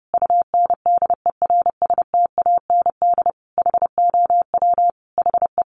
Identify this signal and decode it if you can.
Morse code, heard as a keyed tone, about 30 wpm, 700 Hz. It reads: UNDERSTAND HOW HE
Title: Sentences from Top 600 Words 30wpm